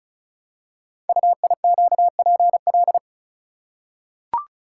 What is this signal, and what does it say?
Morse code, heard as a keyed tone, about 36 wpm, 700 Hz. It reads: UIQPL
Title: Code Groups - 5 Letters - 35wpm